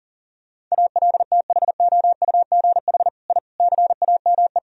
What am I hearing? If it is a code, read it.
Morse code, heard as a keyed tone, about 40 wpm, 700 Hz. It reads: ALTHOUGH I CAME
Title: Sentences from Top 1000 Words - Review Part 3 - 40wpm